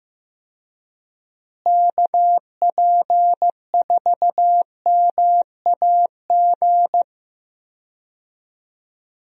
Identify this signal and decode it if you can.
Morse code, heard as a keyed tone, about 15 wpm, 700 Hz. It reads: KP4MAG
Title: USA Call Signs - 6 Characters - Repeated 3x - 15wpm